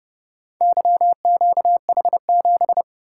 Morse code, heard as a keyed tone, about 29 wpm, 700 Hz. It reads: YQH7